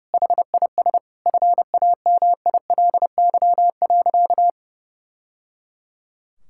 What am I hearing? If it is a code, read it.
Morse code, heard as a keyed tone, about 30 wpm, 700 Hz. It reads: HIS FAMILY.
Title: Sentences from Top 1000 Words - Review Part 5 - 30wpm